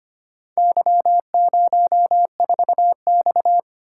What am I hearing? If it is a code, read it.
Morse code, heard as a keyed tone, about 25 wpm, 700 Hz. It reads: Y04X
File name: practice / 25wpm / Four Letter-Number Combinations - 25wpm